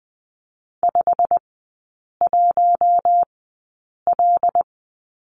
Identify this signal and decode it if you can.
Morse code, heard as a keyed tone, about 20 wpm, spaced as for 8 wpm, 700 Hz. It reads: 51L
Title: 133 Lesson 22e - Question Mark - Three Characters 20wpm